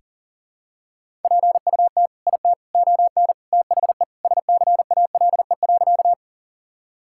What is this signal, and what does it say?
Morse code, heard as a keyed tone, about 40 wpm, 700 Hz. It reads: PUT IT ON THE SCALE.
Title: Sentences from Top 1000 Words - Review Part 5 - 40wpm